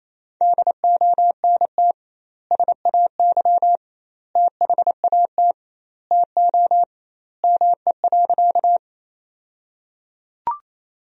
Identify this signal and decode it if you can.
Morse code, heard as a keyed tone, about 28 wpm, 700 Hz. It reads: DONT SAY THAT TO ME.
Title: 2x Word-Spacing - Sentences from Top 300 Words - 28wpm